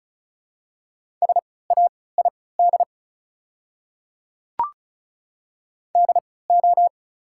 Morse code, heard as a keyed tone, about 34 wpm, spaced as for 20 wpm, 700 Hz. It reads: SAID DO
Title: 3x Character Spacing - Top 500 Words - 35wpm